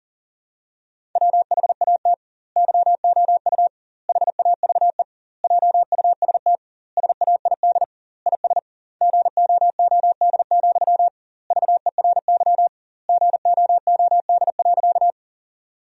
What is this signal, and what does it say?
Morse code, heard as a keyed tone, about 37 wpm, 700 Hz. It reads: WHAT YOU HAVE JUST SAID IS GOOD, VERY GOOD.
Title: 2x Word-Spacing - Sentences from Top 200 Words - 40wpm